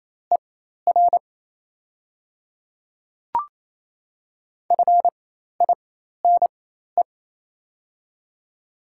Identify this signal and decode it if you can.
Morse code, heard as a keyed tone, about 28 wpm, spaced as for 13 wpm, 700 Hz. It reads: ER FINE
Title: 4x Character Spacing - Top 500 Words - 28wpm